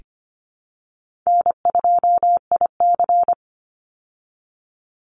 Morse code, heard as a keyed tone, about 25 wpm, 700 Hz. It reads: N2IC
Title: Easy Call Signs Repeated 3x - 25wpm